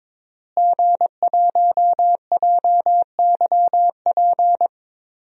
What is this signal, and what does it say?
Morse code, heard as a keyed tone, about 22 wpm, 700 Hz. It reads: G1JYP
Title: DXCC-100 Call Signs - 5 Characters - Repeated 3x - 22wpm